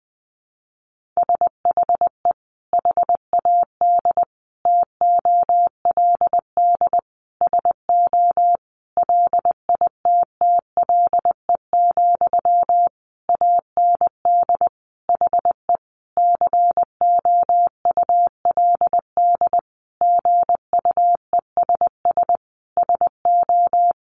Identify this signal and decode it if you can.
Morse code, heard as a keyed tone, about 20 wpm, 700 Hz. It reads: SHE HAD TOLD SO LITTLE, AND HE COULD GUESS SO
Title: Sentences from Top 1500 Words 20wpm